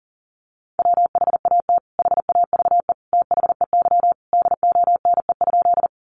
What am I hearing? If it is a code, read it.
Morse code, heard as a keyed tone, about 40 wpm, 700 Hz. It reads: WHAT HAVE THEY DONE?
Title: Sentences from Top 500 Words 40wpm